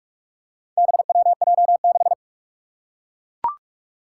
Morse code, heard as a keyed tone, about 41 wpm, 700 Hz. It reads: BWJ6
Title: Four Challenging Letter-Number Combinations 45wpm